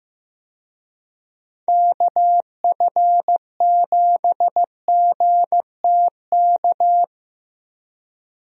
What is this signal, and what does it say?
Morse code, heard as a keyed tone, about 16 wpm, 700 Hz. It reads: KF7GTK
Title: USA Call Signs - 6 Characters - Repeated 2x - 15wpm